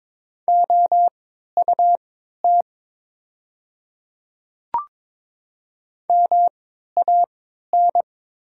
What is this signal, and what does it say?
Morse code, heard as a keyed tone, about 22 wpm, spaced as for 12 wpm, 700 Hz. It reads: OUT MAN
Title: 3x Character Spacing - Top 200 Words - 22wpm